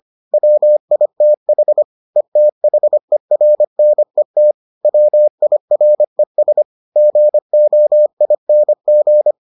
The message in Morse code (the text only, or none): WITH ETHERNET WIRES GOING